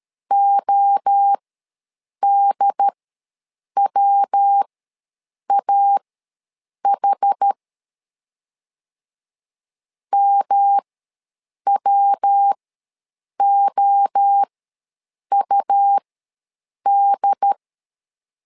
Morse code, heard as a keyed tone, about 13 wpm, spaced as for 7 wpm, 800 Hz. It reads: ODWAH MWOUD
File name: LEZIONI / 3 MWRO / 3 MWRO ws 12wpm 1914